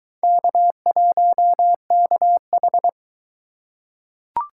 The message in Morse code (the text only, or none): K1KH